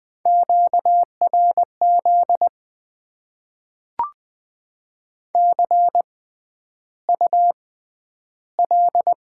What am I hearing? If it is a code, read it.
Morse code, heard as a keyed tone, about 20 wpm, spaced as for 10 wpm, 700 Hz. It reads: QRZ CUL
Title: Top 90 QSO Elements - 6x Character Spacing - 20wpm